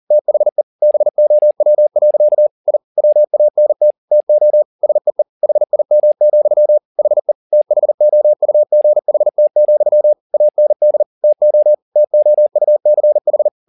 Morse code, heard as a keyed tone, about 40 wpm, 600 Hz. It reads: THE BOW. I WANT TO SEE HIM, HE THOUGHT, AND TO TOUCH
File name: the old man and the sea 2 40wpm